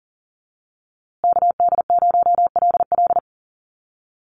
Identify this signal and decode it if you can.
Morse code, heard as a keyed tone, about 40 wpm, 700 Hz. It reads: KD0LL
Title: USA Call Signs Repeated 3x 40wpm